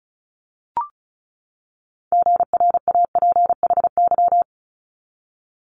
Morse code, heard as a keyed tone, about 36 wpm, 700 Hz. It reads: GRAPHY
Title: Multi-letter Phonemes 35wpm